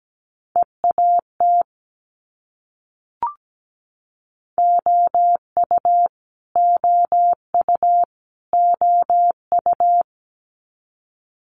Morse code, heard as a keyed tone, about 17 wpm, 700 Hz. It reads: EAT OU OU OU
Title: Multi-letter Phonemes Repeated 3x 17wpm